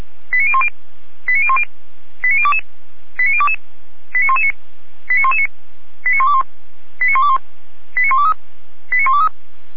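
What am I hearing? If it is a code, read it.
Five-tone selective call (ZVEI), heard as quick runs of five tones, tones of about 70 ms. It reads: 89010 89010 89022 89022 89119 89119 89121 89121 89123 89123
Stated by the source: ZVEI five-tone call received on SDR.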